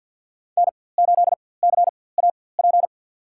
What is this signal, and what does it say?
Morse code, heard as a keyed tone, about 47 wpm, spaced as for 23 wpm, 700 Hz. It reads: N8CAP
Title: USA Call Signs - 4x Character Spacing - 50wpm